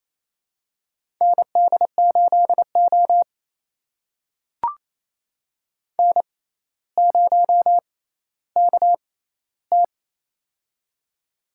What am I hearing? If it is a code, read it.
Morse code, heard as a keyed tone, about 28 wpm, spaced as for 14 wpm, 700 Hz. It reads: ND8O N0KT
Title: USA Call Signs - 4 Characters - 6x Character Spacing - 28wpm